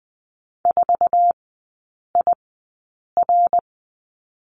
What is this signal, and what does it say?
Morse code, heard as a keyed tone, about 20 wpm, spaced as for 8 wpm, 700 Hz. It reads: 4IR
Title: 093 Lesson 16e - Letter c - Three Characters 20wpm